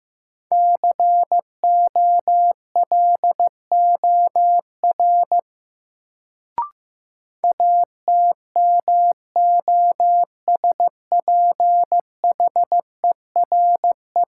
Morse code, heard as a keyed tone, about 16 wpm, 700 Hz. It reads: COLOR ATMOSPHERE
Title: Sentences from Top 1600 Words - New Word Drill Alt - 15wpm